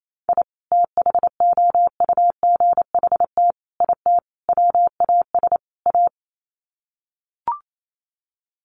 Morse code, heard as a keyed tone, about 28 wpm, 700 Hz. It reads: I THOUGHT IT WAS A
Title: Sets of 5 Words 28wpm